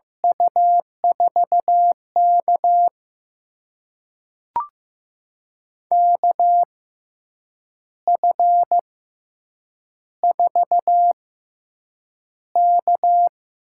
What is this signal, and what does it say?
Morse code, heard as a keyed tone, about 15 wpm, spaced as for 7 wpm, 700 Hz. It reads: U4K KF4K
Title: USA Call Signs - 6x Character Spacing - 15wpm